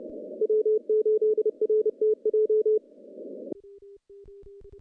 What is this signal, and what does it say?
Morse code, heard as a keyed tone, about 28 wpm, 440 Hz. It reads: W8RTJ
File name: contests / CW-SS-k5tr- -0400